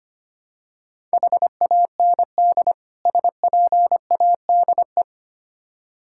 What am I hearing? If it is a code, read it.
Morse code, heard as a keyed tone, about 28 wpm, 700 Hz. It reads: HAND SPADE
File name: Easy Words - Speed-Racing with Hints - 25wpm